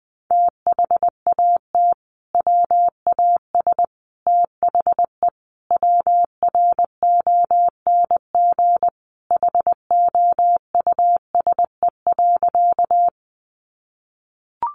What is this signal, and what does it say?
Morse code, heard as a keyed tone, about 20 wpm, 700 Hz. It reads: THAT WAS THE WRONG HOUSE.
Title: Sentences from Top 900 Words 20wpm